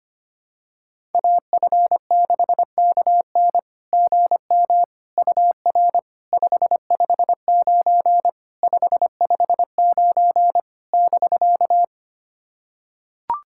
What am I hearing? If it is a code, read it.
Morse code, heard as a keyed tone, about 25 wpm, 690 Hz. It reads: AF6KN GM UR 559 559 <BK>
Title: SOTA QSO Practice 25wpm